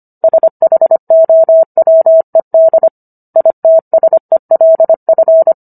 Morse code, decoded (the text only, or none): SHOWED ITSELF